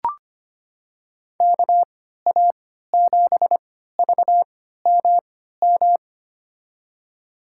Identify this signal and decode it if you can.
Morse code, heard as a keyed tone, about 25 wpm, spaced as for 14 wpm, 700 Hz. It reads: KA7VMM
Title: USA Call Signs - 3x Character Spacing - 25wpm